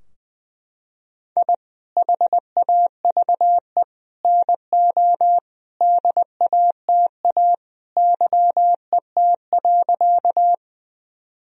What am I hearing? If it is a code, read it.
Morse code, heard as a keyed tone, about 20 wpm, 700 Hz. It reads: I HAVE NO DATA YET.
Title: Sentences from Top 1000 Words - Review Part 3 - 20wpm